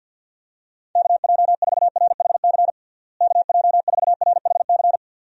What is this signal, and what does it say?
Morse code, heard as a keyed tone, about 51 wpm, 700 Hz. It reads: KJ4RHC KJ4RHC
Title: USA Call Signs - 6 Characters - Repeated 2x - 50wpm